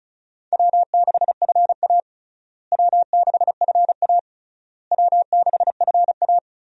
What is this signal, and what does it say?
Morse code, heard as a keyed tone, about 34 wpm, 700 Hz. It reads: W6FA W6FA W6FA
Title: USA Call Signs - 4 Characters - Repeated 3x - 35wpm